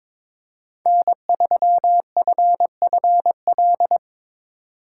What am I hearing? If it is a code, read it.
Morse code, heard as a keyed tone, about 22 wpm, 700 Hz. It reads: N3FFL
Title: Speed Racing - USA 1x3 Call Signs 22wpm